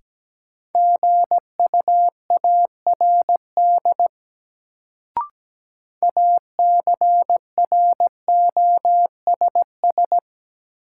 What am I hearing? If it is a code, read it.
Morse code, heard as a keyed tone, about 17 wpm, 700 Hz. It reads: GUARD ACROSS
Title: Sentences from Top 700 Words - New Word Drill - 17wpm